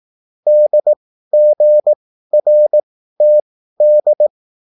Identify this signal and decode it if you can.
Morse code, heard as a keyed tone, about 18 wpm, spaced as for 13 wpm, 600 Hz. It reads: DGRTD